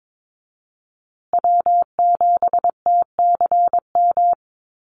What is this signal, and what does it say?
Morse code, heard as a keyed tone, about 22 wpm, 700 Hz. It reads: W7TCM
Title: USA Call Signs Repeated 2x 22wpm